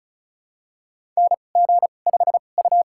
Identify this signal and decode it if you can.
Morse code, heard as a keyed tone, about 36 wpm, spaced as for 25 wpm, 700 Hz. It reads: NG5U
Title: USA Call Signs - 4 Characters - 2x Character Spacing - 35wpm